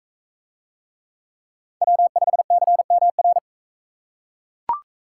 Morse code, heard as a keyed tone, about 40 wpm, 700 Hz. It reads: W5CMR